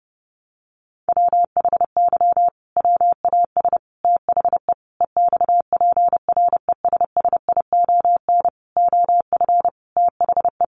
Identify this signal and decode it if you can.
Morse code, heard as a keyed tone, about 30 wpm, 700 Hz. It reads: WHY WAS THE EXPRESSION OF THE
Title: Sentences from Top 1400 Words 30wpm